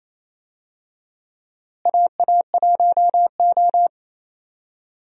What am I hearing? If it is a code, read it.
Morse code, heard as a keyed tone, about 28 wpm, 700 Hz. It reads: AA1O
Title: USA Call Signs - 4 Characters - Repeated 2x - 28wpm